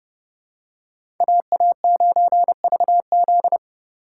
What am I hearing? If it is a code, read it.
Morse code, heard as a keyed tone, about 30 wpm, 700 Hz. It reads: AA9VZ
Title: USA Call Signs - 3x Character Spacing - 30wpm